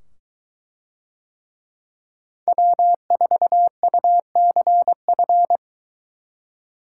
Morse code, heard as a keyed tone, about 22 wpm, 700 Hz. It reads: W4UCF